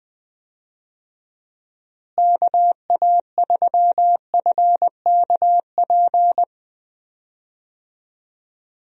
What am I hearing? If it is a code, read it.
Morse code, heard as a keyed tone, about 20 wpm, 700 Hz. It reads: KA3FKP
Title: USA Call Signs - 6 Characters - Repeated 3x - 20wpm